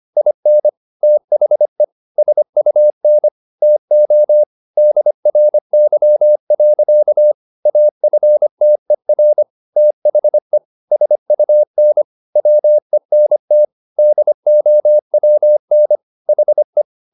Morse code, heard as a keyed tone, about 25 wpm, 600 Hz. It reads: IN THE SUN TO DRY. AFTER THE SUN WENT DOWN HE
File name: the old man and the sea part 1 25wpm